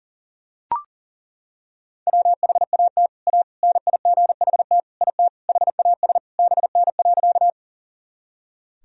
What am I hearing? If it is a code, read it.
Morse code, heard as a keyed tone, about 40 wpm, 700 Hz. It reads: WHAT A NIGHT IT HAS BN.
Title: Sentences from Top 300 Words and Abbreviations 40wpm